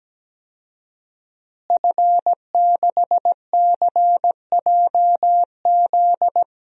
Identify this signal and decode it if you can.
Morse code, heard as a keyed tone, about 17 wpm, 700 Hz. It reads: F6CJZ
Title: DXCC-100 Call Signs - 5 Characters - Repeated 3x - 17wpm